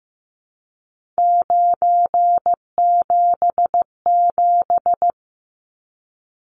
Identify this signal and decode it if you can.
Morse code, heard as a keyed tone, about 15 wpm, 700 Hz. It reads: 977
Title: Numbers - 3-Digit 15wpm